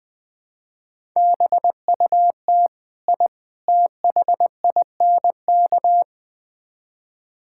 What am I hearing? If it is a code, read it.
Morse code, heard as a keyed tone, about 20 wpm, 700 Hz. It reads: BUT I THINK
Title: Build A Phrase - 4-Word Phrases - 20wpm